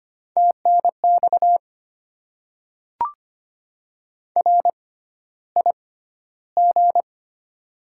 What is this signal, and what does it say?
Morse code, heard as a keyed tone, about 25 wpm, spaced as for 13 wpm, 700 Hz. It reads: TNX RIG
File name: Top 20 QSO Elements - 6x Character Spacing - 25wpm